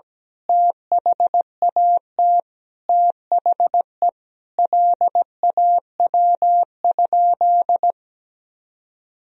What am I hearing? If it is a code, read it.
Morse code, heard as a keyed tone, about 17 wpm, 700 Hz. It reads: THAT THE LAW?
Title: Sentences from Top 1000 Words - Review Part 3 - 17wpm